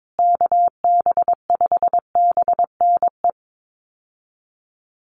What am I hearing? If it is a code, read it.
Morse code, heard as a keyed tone, about 22 wpm, 700 Hz. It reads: KB5BNE
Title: USA Call Signs Repeated 3x 22wpm